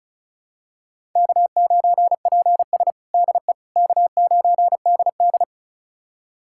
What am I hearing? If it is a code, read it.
Morse code, heard as a keyed tone, about 36 wpm, 700 Hz. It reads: K9PS DE K9DD
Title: USA Callsign DE Callsign - 4 Characters - 35wpm